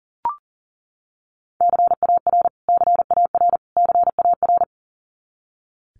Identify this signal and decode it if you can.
Morse code, heard as a keyed tone, about 40 wpm, 700 Hz. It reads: CAR CAR CAR
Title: Top 300 Words Repeated 40wpm